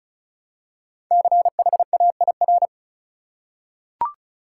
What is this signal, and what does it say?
Morse code, heard as a keyed tone, about 36 wpm, 700 Hz. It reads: CHAIR
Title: Five Letter Words - 35wpm